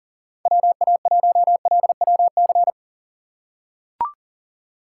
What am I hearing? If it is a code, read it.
Morse code, heard as a keyed tone, about 40 wpm, 690 Hz. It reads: WA1LWC
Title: K1USN SST QSO Practice - 40wpm